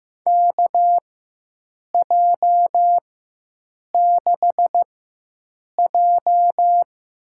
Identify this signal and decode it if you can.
Morse code, heard as a keyed tone, about 15 wpm, spaced as for 7 wpm, 700 Hz. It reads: KJ6J
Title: USA Call Signs - 6 Characters - 4x Character Spacing - 15wpm